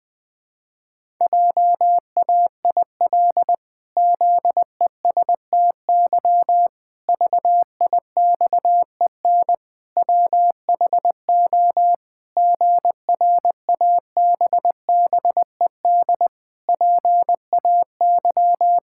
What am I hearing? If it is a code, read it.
Morse code, heard as a keyed tone, about 21 wpm, 700 Hz. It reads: JAIL ZESTY VIXEN WHO GRABBED PAY
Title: Pangram Challenge v2 - 20wpm